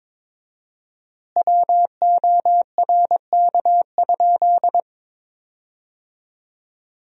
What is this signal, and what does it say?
Morse code, heard as a keyed tone, about 22 wpm, 700 Hz. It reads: WORK?
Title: 4x Word-Spacing - Sentences from Top 500 Words - 22wpm